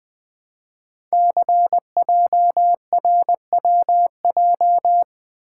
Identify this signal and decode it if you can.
Morse code, heard as a keyed tone, about 20 wpm, 700 Hz. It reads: CJRWJ
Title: Code Groups - 5 Letters - 20wpm